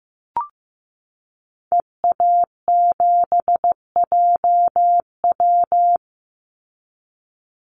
Morse code, heard as a keyed tone, about 15 wpm, 700 Hz. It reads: EA7JW
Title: Call Signs Repeated 2x - 15wpm